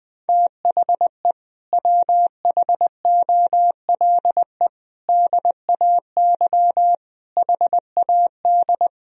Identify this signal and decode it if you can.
Morse code, heard as a keyed tone, about 21 wpm, 700 Hz. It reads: THE WHOLE DAY HAD
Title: Sentences from Top 500 Words and Abbreviations 20wpm